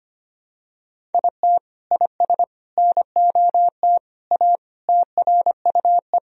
Morse code, heard as a keyed tone, about 25 wpm, 700 Hz. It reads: IT IS NOT A TRUE
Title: Sentences from Top 1000 Words - Review Part 4 - 25wpm